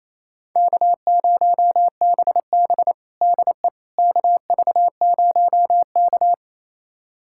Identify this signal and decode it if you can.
Morse code, heard as a keyed tone, about 28 wpm, 700 Hz. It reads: K0BB DE KV0K
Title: USA Callsign DE Callsign - 4 Characters - 28wpm